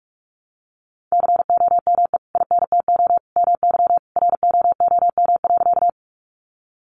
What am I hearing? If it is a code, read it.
Morse code, heard as a keyed tone, about 44 wpm, 700 Hz. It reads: COME INTO MY ROOM.
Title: Sentences from Top 600 Words 45wpm